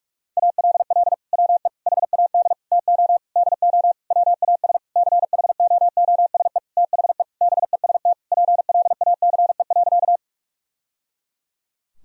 Morse code, heard as a keyed tone, about 46 wpm, 700 Hz. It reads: ALL WE HAD TO DO WAS CHOOSE THE BEST PLACE.
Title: Sentences from Top 1000 Words - Review Part 4 - 45wpm